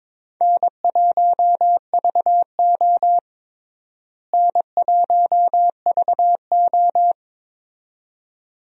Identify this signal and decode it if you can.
Morse code, heard as a keyed tone, about 22 wpm, 700 Hz. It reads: N1VO N1VO
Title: USA Call Signs - 4 Characters - Repeated 2x - 22wpm